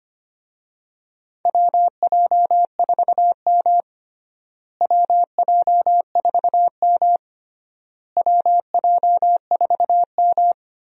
Morse code, heard as a keyed tone, about 25 wpm, 700 Hz. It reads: WJ4M WJ4M WJ4M
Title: USA Call Signs - 4 Characters - Repeated 3x - 25wpm